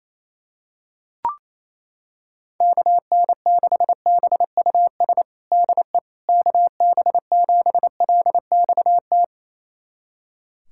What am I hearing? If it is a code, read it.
Morse code, heard as a keyed tone, about 28 wpm, 700 Hz. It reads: KN6BUS DE KB7LXT
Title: USA Callsign DE Callsign - 6 Characters - 28wpm